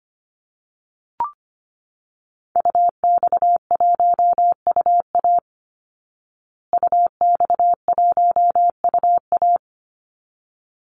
Morse code, heard as a keyed tone, about 25 wpm, 700 Hz. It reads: UX1UA UX1UA
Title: Call Signs Repeated 3x - 25wpm